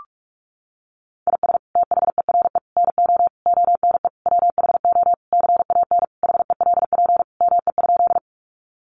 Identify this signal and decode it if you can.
Morse code, heard as a keyed tone, about 44 wpm, 700 Hz. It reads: IS THERE NO ONE WHO CAN HELP ME?
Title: Sentences from Top 200 Words 45wpm